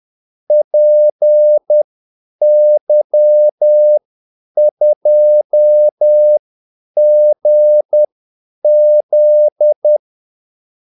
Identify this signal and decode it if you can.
Morse code, heard as a keyed tone, about 10 wpm, spaced as for 8 wpm, 600 Hz. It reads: PY2GZ
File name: Callsigns-10-8wpm